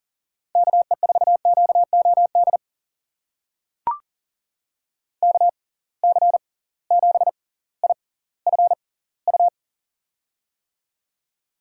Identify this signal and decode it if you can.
Morse code, heard as a keyed tone, about 40 wpm, spaced as for 21 wpm, 700 Hz. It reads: KE4QOD KC7IFU
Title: USA Call Signs - 6 Characters - 6x Character Spacing - 40wpm